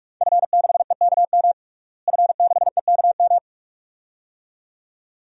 Morse code, heard as a keyed tone, about 46 wpm, 700 Hz. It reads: F6EKM F6EKM
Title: DXCC-100 Call Signs - 5 Characters - Repeated 2x - 45wpm